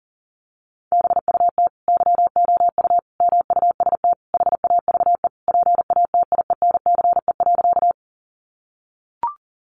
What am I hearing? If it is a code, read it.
Morse code, heard as a keyed tone, about 37 wpm, 700 Hz. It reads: BUT YOU MUST HAVE PATIENCE.
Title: Sentences from Top 1300 Words 40wpm